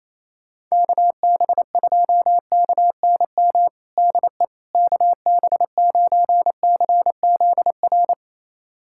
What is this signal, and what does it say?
Morse code, heard as a keyed tone, about 28 wpm, 700 Hz. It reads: KB2KNM DE KB9CZR
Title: USA Callsign DE Callsign - 6 Characters - 28wpm